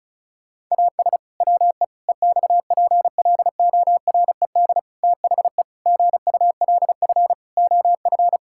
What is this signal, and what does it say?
Morse code, heard as a keyed tone, about 36 wpm, 700 Hz. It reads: AS WE EXPLORED THE GULF OF
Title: Pangram Challenge v2 - 35wpm